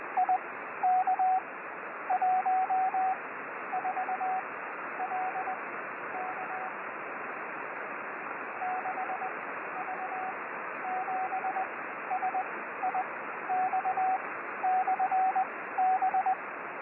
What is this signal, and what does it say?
Morse code, heard as a keyed tone, about 21 wpm, spaced as for 12 wpm, 750 Hz, with radio noise. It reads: IK14LK 6V7SIX/B
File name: 6V7SIX B